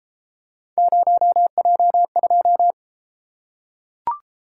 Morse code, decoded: 0J2

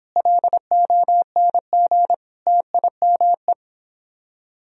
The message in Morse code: LONG TIME

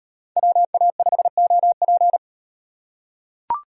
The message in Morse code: WA5OP

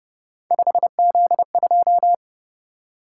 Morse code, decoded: 5Z2

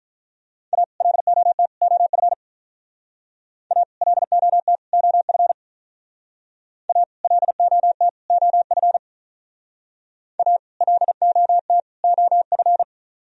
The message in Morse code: A LOT OF A LOT OF A LOT OF A LOT OF

